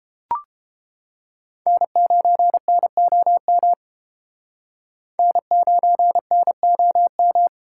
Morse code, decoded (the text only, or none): N9NOM N9NOM